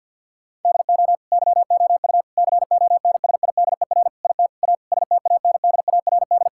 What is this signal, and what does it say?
Morse code, heard as a keyed tone, about 50 wpm, 700 Hz. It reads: DO YOU CONSIDER IT A STANDARD
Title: Sentences from Top 1000 Words - Review Part 6 - 50wpm